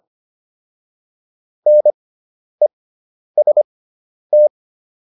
Morse code, heard as a keyed tone, about 25 wpm, spaced as for 10 wpm, 600 Hz. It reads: NEST